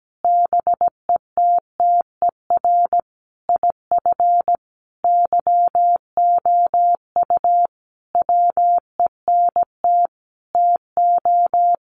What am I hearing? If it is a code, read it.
Morse code, heard as a keyed tone, about 17 wpm, 700 Hz. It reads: BETTER IF YOU WENT TO
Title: Sentences from Top 400 Words 17wpm